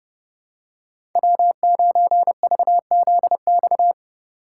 Morse code, heard as a keyed tone, about 30 wpm, 700 Hz. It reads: W9VZX
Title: USA Call Signs - 5 Characters - 3x Character Spacing - 30wpm